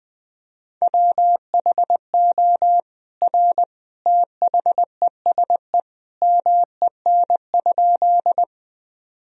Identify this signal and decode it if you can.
Morse code, heard as a keyed tone, about 21 wpm, 700 Hz. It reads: WHO R THESE MEN?
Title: Sentences from Top 200 Words and Abbreviations 20wpm